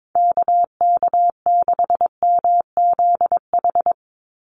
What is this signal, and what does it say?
Morse code, heard as a keyed tone, about 22 wpm, 700 Hz. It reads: KK6MZH
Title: USA Call Signs 22wpm